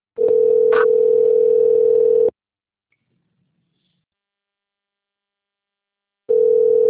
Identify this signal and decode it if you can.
Ringback tone (North America), 440 + 480 Hz.